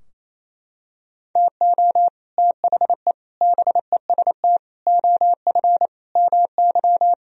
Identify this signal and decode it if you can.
Morse code, heard as a keyed tone, about 28 wpm, 700 Hz. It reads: TO THE BEST OF MY
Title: Sentences from Top 1000 Words - Review Part 3 - 28wpm